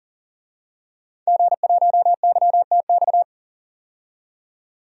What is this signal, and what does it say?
Morse code, heard as a keyed tone, about 39 wpm, 700 Hz. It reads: G1YTX
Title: DXCC-100 Call Signs - 5 Characters - 40wpm